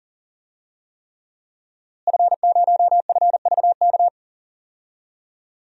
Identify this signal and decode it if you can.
Morse code, heard as a keyed tone, about 39 wpm, 700 Hz. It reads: F0FVK
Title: DXCC-100 Call Signs - 5 Characters - Repeated 3x - 40wpm